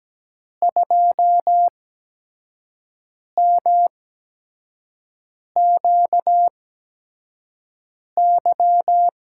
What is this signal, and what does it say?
Morse code, heard as a keyed tone, about 17 wpm, spaced as for 5 wpm, 700 Hz. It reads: 2MQY
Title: USA Call Signs - 5 Characters - 8x Character Spacing - 17wpm